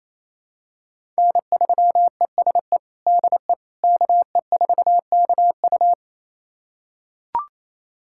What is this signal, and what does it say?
Morse code, heard as a keyed tone, about 28 wpm, 700 Hz. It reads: N3ESE DE KE4KU
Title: USA Callsign DE Callsign - 5 Characters - 28wpm